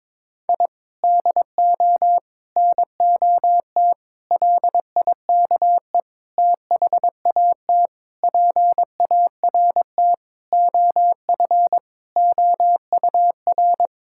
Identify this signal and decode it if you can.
Morse code, heard as a keyed tone, about 22 wpm, 700 Hz. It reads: I DO NOT LIKE THAT PART OF OUR
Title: Sentences from Top 1000 Words - Review Part 6 - 22wpm